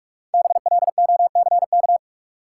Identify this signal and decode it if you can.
Morse code, heard as a keyed tone, about 46 wpm, 700 Hz. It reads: BLOCK
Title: Sentences from Top 1600 Words - New Word Drill - 45wpm